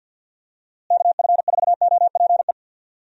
Morse code, heard as a keyed tone, about 51 wpm, 700 Hz. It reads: KF4OPE
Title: USA Call Signs - 6 Characters - Repeated 2x - 50wpm